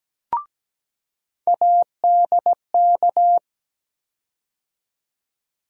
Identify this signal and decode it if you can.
Morse code, heard as a keyed tone, about 17 wpm, 700 Hz. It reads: ADK
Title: USA Call Sign Postfixes - 3 Letters 17wpm